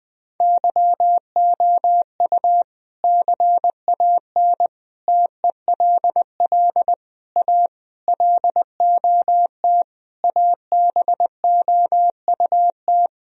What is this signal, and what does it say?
Morse code, heard as a keyed tone, about 20 wpm, 700 Hz. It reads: YOU CAN TELL A LOT ABOUT
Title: Sentences from Top 1000 Words - Review Part 3 - 20wpm